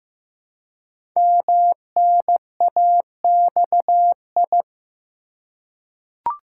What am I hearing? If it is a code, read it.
Morse code, heard as a keyed tone, about 15 wpm, 700 Hz. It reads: MNAXI
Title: Code Groups - 5 Letters - 15wpm